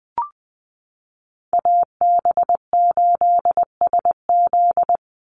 Morse code, heard as a keyed tone, about 20 wpm, 700 Hz. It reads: AB8SZ
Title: USA Call Signs 20wpm